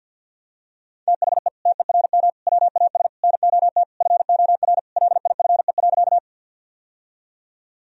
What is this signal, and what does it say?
Morse code, heard as a keyed tone, about 47 wpm, 700 Hz. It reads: THE TERM WAS NOT FOR LIFE.
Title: Sentences from Top 1000 Words - Review Part 4 - 50wpm